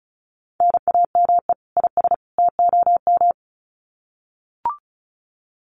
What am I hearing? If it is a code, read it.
Morse code, heard as a keyed tone, about 34 wpm, 700 Hz. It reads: NAME IS TOM
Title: Common QSO Phrases 35wpm